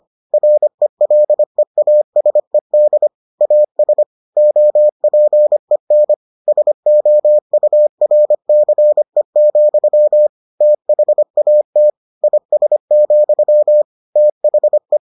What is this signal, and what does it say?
Morse code, heard as a keyed tone, about 25 wpm, 600 Hz. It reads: RELEASED AS OPEN SOURCE, THAT IS, THE